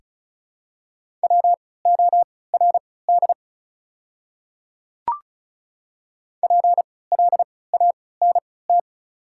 Morse code, heard as a keyed tone, about 36 wpm, spaced as for 20 wpm, 700 Hz. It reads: WORD PLANT
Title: 3x Character Spacing - Top 300 Words - 35wpm